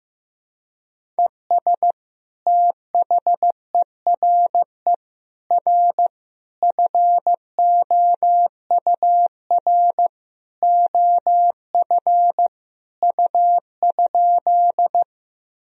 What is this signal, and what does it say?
Morse code, heard as a keyed tone, about 15 wpm, 700 Hz. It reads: ES THERE R FOUR OF U?
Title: Sentences from Top 300 Words and Abbreviations 15wpm